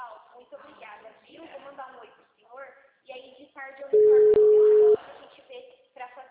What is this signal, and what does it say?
Ringback tone (Europe, 425 Hz), with speech.
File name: out- -2011- - - .8695